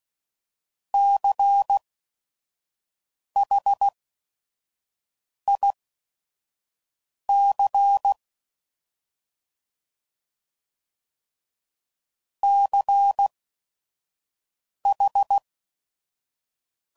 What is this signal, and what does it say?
Morse code, heard as a keyed tone, about 16 wpm, spaced as for 5 wpm, 800 Hz. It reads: CHIC CH
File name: Prefixes / By First Letter / Cs